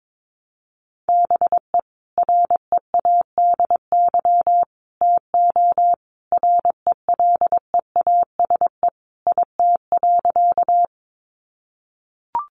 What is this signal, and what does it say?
Morse code, heard as a keyed tone, about 22 wpm, 700 Hz. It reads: BE READY TO RELEASE IT.
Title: Sentences from Top 1100 Words 22wpm